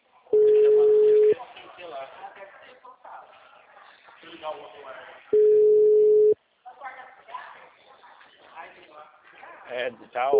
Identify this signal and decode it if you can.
Ringback tone (Europe, 425 Hz), with speech.